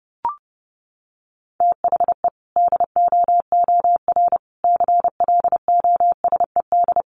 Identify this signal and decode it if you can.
Morse code, heard as a keyed tone, about 28 wpm, 700 Hz. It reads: THE DOOR CLOSED
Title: Sentences from Top 1400 Words 30wpm